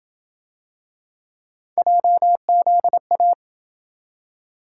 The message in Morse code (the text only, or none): JZA